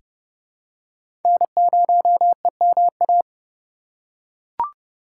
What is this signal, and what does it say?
Morse code, heard as a keyed tone, about 30 wpm, 700 Hz. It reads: N0EMA
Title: USA Call Signs - 5 Characters - 3x Character Spacing - 30wpm